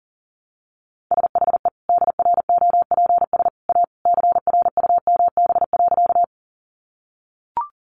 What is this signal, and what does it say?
Morse code, heard as a keyed tone, about 40 wpm, 700 Hz. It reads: SHE DROPS A CRUMB.
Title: Additional Easy Phrases 40wpm